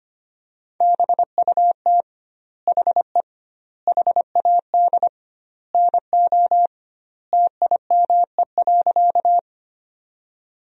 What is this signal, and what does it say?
Morse code, heard as a keyed tone, about 25 wpm, 700 Hz. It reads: BUT HE HAD NO TIME.
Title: 2x Word-Spacing - Sentences from Top 100 Words - 25wpm